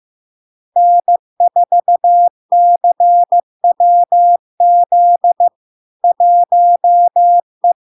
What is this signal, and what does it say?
Morse code, heard as a keyed tone, about 15 wpm, 700 Hz. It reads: N4CWZ 1E
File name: LARK-FD-2022-Log-in-CW-15wpm-700-Hz